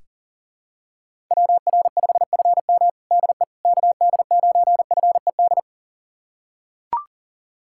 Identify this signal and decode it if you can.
Morse code, heard as a keyed tone, about 40 wpm, 700 Hz. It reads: WR5FM DE KD9FED
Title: USA Callsign DE Callsign 40wpm